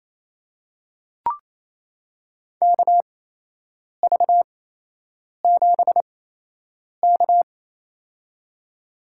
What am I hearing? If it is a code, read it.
Morse code, heard as a keyed tone, about 28 wpm, spaced as for 8 wpm, 700 Hz. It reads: KV7K
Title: USA Call Signs - 4 Characters - 8x Character Spacing - 28wpm